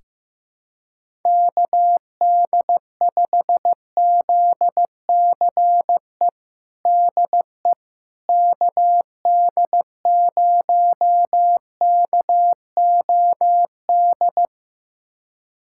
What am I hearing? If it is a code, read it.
Morse code, heard as a keyed tone, about 15 wpm, 700 Hz. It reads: KD5ZCE DE KD0KOD
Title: USA Callsign DE Callsign - 6 Characters - 15wpm